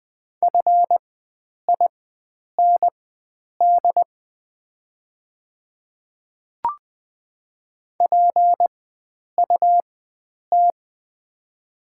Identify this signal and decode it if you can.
Morse code, heard as a keyed tone, about 20 wpm, spaced as for 9 wpm, 690 Hz. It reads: FIND PUT
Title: 4x Character Spacing - Top 500 Words - 20wpm